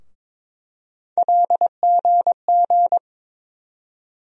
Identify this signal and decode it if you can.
Morse code, heard as a keyed tone, about 22 wpm, 700 Hz. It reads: LGG
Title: USA Call Sign Postfixes - 3 Letters 22wpm